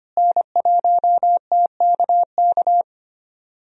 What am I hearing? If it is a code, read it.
Morse code, heard as a keyed tone, about 25 wpm, 700 Hz. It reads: N1TKK
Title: USA Call Signs - 5 Characters - Repeated 3x - 25wpm